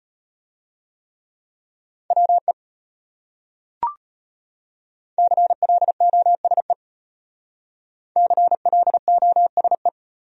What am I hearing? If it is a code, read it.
Morse code, heard as a keyed tone, about 33 wpm, 700 Hz. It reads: WE CLOSE CLOSE